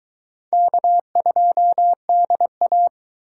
Morse code, heard as a keyed tone, about 22 wpm, 700 Hz. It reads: K2DA